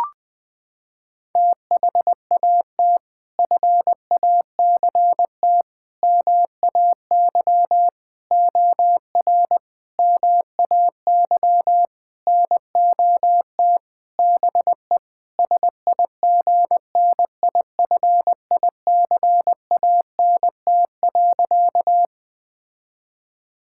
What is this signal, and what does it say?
Morse code, heard as a keyed tone, about 20 wpm, 700 Hz. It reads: THAT FACT MAY OR MAY NOT BE SIGNIFICANT.
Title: Sentences from Top 1000 Words - Review Part 6 - 20wpm